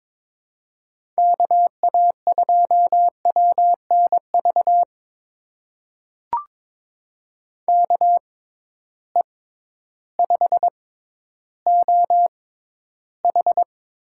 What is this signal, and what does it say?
Morse code, heard as a keyed tone, about 22 wpm, spaced as for 12 wpm, 700 Hz. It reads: KA2WNV KE5OH
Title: USA Call Signs - 6 Characters - 6x Character Spacing - 22wpm